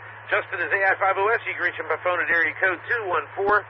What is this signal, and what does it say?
Emergency Alert System (SAME) tones, heard as harsh buzzing bursts, with speech. It is heard from 0.2 to 2.8 s.